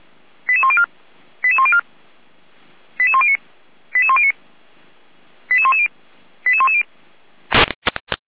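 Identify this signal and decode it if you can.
Five-tone selective call (ZVEI), heard as quick runs of five tones, tones of about 70 ms. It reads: 88104 88104 88109 88109 88110 88110
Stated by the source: Five-tone ZVEI selective call received by SDR, from the audio archive of Feuerwehr Orlamünde.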